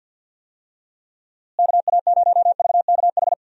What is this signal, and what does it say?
Morse code, heard as a keyed tone, about 52 wpm, 700 Hz. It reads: KA0VKH
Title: USA Call Signs - 6 Characters - Repeated 3x - 50wpm